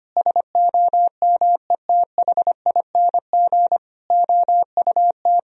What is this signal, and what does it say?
Morse code, heard as a keyed tone, about 25 wpm, 700 Hz. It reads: SOMETHING OUT
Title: Sentences from Top 1000 Words - Review Part 3 - 25wpm